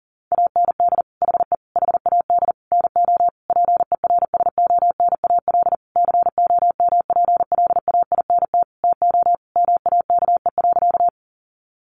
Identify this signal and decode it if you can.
Morse code, heard as a keyed tone, about 40 wpm, 700 Hz. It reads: AND HE HAD NO PERSONAL COMPLAINT TO MAKE.
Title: Sentences from Top 1400 Words 40wpm